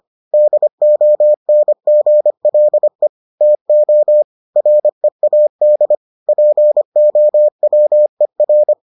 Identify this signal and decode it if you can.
Morse code, heard as a keyed tone, about 25 wpm, 600 Hz. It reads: DONGLE TO READ POWER